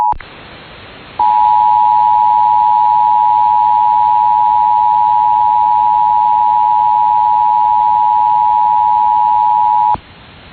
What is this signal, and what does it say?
Emergency Alert System (SAME) tones (the 853 and 960 Hz attention tone). It is heard from 1.1 to 10.0 s.